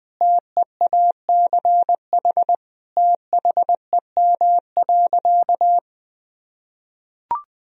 Morse code, heard as a keyed tone, about 20 wpm, 700 Hz. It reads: TEACH THEM.
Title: Sentences from Top 1000 Words - Review Part 2 - 20wpm